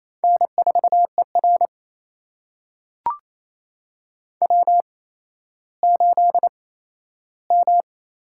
Morse code, heard as a keyed tone, about 28 wpm, spaced as for 14 wpm, 700 Hz. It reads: N4ER W8M
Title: USA Call Signs - 8x Character Spacing - 28wpm